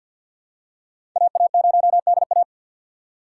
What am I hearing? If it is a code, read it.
Morse code, heard as a keyed tone, about 47 wpm, 700 Hz. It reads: AA0DA
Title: USA Call Signs - 5 Characters - 6x Character Spacing - 50wpm